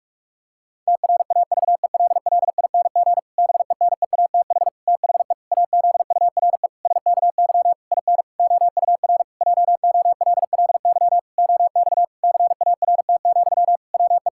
Morse code, heard as a keyed tone, about 46 wpm, 700 Hz. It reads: TRAVELLING BENEATH THE AZURE SKY IN OUR JOLLY OX CART, WE
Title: Pangram Challenge v2 - 45wpm